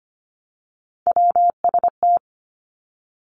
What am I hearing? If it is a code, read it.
Morse code, heard as a keyed tone, about 25 wpm, 700 Hz. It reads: WST
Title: Three Challenging Letter-Number Combinations 25wpm